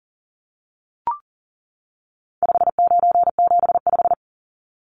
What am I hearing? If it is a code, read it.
Morse code, heard as a keyed tone, about 40 wpm, 700 Hz. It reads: 5975
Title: Numbers - 4-Digit 40wpm